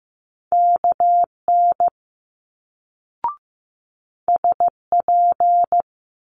Morse code, heard as a keyed tone, about 15 wpm, 700 Hz. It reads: KN SP
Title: Multi-letter Phonemes Quick Fire 15wpm